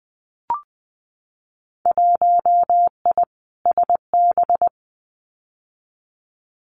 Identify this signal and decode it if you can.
Morse code, heard as a keyed tone, about 20 wpm, 700 Hz. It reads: 1I SB
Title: Winter Field Day Exchanges 20wpm